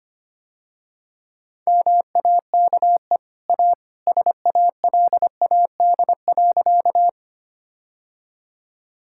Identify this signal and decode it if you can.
Morse code, heard as a keyed tone, about 25 wpm, 700 Hz. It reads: MAKE A SALAD.
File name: Easy Sentences with Hints - 25wpm